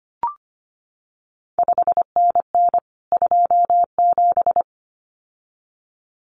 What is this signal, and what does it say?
Morse code, heard as a keyed tone, about 25 wpm, 700 Hz. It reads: 5NN 27
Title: CQ WW DX Contest Exchange Encore - 25wpm